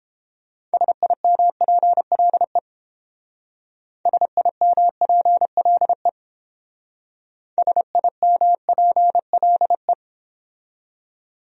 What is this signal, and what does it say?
Morse code, heard as a keyed tone, about 28 wpm, 700 Hz. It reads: SIMPLE SIMPLE SIMPLE